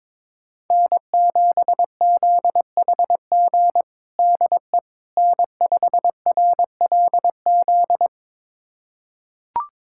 Morse code, heard as a keyed tone, about 22 wpm, 700 Hz. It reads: N7ZHG DE N5RLZ
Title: USA Callsign DE Callsign - 5 Characters - 22wpm